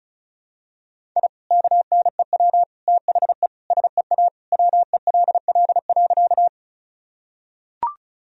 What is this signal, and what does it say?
Morse code, heard as a keyed tone, about 36 wpm, 700 Hz. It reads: I KNEW THE SEA WELL.
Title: Sentences from Top 1000 Words - Review Part 1 - 35wpm